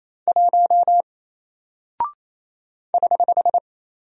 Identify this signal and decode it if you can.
Morse code, heard as a keyed tone, about 28 wpm, 700 Hz. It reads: 1 <HH>
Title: Single Character (Letter:Number:Punctuation:Prosign) Alt 28wpm